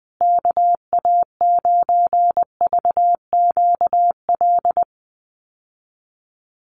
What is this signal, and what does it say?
Morse code, heard as a keyed tone, about 20 wpm, 700 Hz. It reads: KA9VQL
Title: USA Call Signs Repeated 3x 20wpm